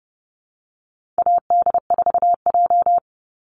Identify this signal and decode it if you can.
Morse code, heard as a keyed tone, about 30 wpm, 700 Hz. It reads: AD4J
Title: Call Signs - 30wpm